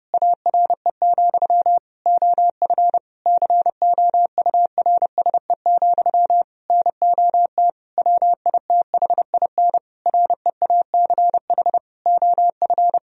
Morse code, heard as a keyed tone, about 30 wpm, 700 Hz. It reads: ARE, OF COURSE, NOT WITHIN REACH OF
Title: Sentences from Top 1000 Words - Review Part 4 - 30wpm